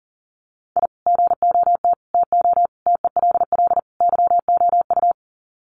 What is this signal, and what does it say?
Morse code, heard as a keyed tone, about 40 wpm, 700 Hz. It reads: I GOT TO TELL YOU
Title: Sets of 5 Words Encore2 40wpm